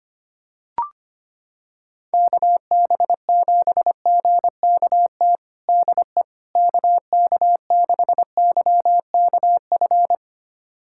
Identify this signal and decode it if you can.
Morse code, heard as a keyed tone, about 25 wpm, 700 Hz. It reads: KB7GKT DE KK6YKF
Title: USA Callsign DE Callsign 25wpm